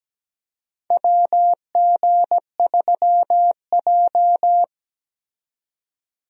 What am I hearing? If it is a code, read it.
Morse code, heard as a keyed tone, about 17 wpm, 700 Hz. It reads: WG3J
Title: USA Call Signs - 4 Characters - 17wpm